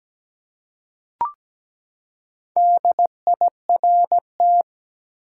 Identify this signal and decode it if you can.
Morse code, heard as a keyed tone, about 17 wpm, 700 Hz. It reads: DIRT
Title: Four Letter Words - 17wpm